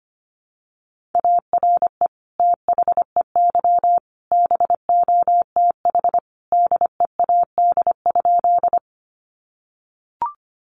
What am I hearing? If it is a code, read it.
Morse code, heard as a keyed tone, about 25 wpm, 700 Hz. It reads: ARE THEY BOTH DEAD?
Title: Sentences from Top 1500 Words 25wpm